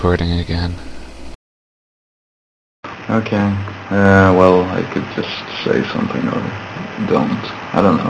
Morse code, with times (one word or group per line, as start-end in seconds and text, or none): none